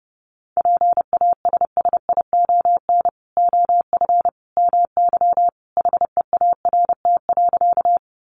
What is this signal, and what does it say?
Morse code, heard as a keyed tone, about 30 wpm, 700 Hz. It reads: PASSION OF MY HEART.
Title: Sentences from Top 1300 Words 30wpm